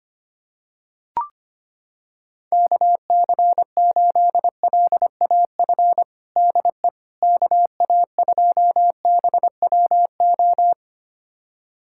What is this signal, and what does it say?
Morse code, heard as a keyed tone, about 25 wpm, 700 Hz. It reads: KC8LAF DE KA2BWO
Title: USA Callsign DE Callsign - 6 Characters - 25wpm